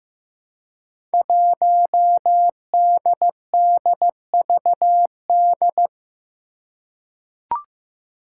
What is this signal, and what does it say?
Morse code, heard as a keyed tone, about 15 wpm, 700 Hz. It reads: 1DDVD
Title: Five Challenging Letter-Number Combinations 15wpm